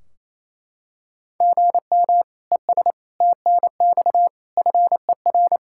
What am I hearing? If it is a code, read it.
Morse code, heard as a keyed tone, about 28 wpm, 700 Hz. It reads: GM ES TNX FER
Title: Practice QSO - Exchange 1 - Beginning - 28wpm